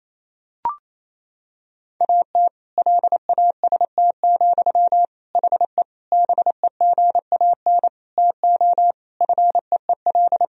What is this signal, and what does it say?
Morse code, heard as a keyed tone, about 28 wpm, 700 Hz. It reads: AT LAST, HE BEGAN TO FEEL
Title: Sentences from Top 1000 Words - Review Part 1 - 28wpm